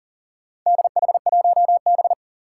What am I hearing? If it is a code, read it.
Morse code, heard as a keyed tone, about 40 wpm, 700 Hz. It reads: DH1B
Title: Four Challenging Letter-Number Combinations 40wpm